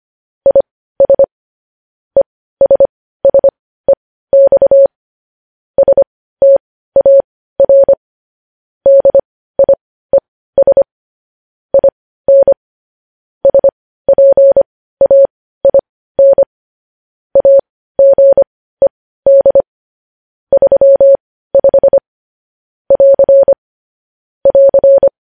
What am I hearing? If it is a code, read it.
Morse code, heard as a keyed tone, about 25 wpm, spaced as for 15 wpm, 550 Hz. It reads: IS ESSEX STAR DIES IN SPAIN AGED 35 <AR> <AR>
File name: podcast / News headlines-1x-25x15